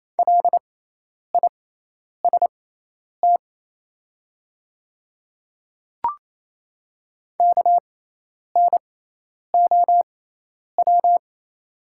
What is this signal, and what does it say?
Morse code, heard as a keyed tone, about 28 wpm, spaced as for 10 wpm, 700 Hz. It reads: LIST KNOW
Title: 6x Character Spacing - Top 500 Words - 28wpm